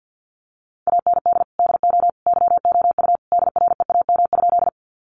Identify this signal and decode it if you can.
Morse code, heard as a keyed tone, about 50 wpm, 700 Hz. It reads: AND DO YOU DREAM?
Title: Sentences from Top 1500 Words 50wpm